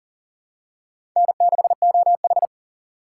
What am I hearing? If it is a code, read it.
Morse code, heard as a keyed tone, about 40 wpm, 700 Hz. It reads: N6OH
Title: USA Call Signs - 4 Characters - 3x Character Spacing - 40wpm